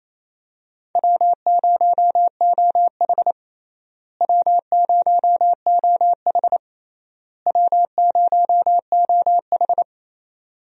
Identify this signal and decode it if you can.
Morse code, heard as a keyed tone, about 28 wpm, 700 Hz. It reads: W0OH W0OH W0OH
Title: USA Call Signs - 4 Characters - Repeated 3x - 28wpm